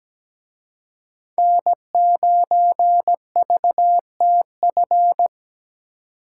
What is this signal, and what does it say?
Morse code, heard as a keyed tone, about 20 wpm, 700 Hz. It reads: N9VTF
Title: Speed Racing - USA 1x3 Call Signs 17wpm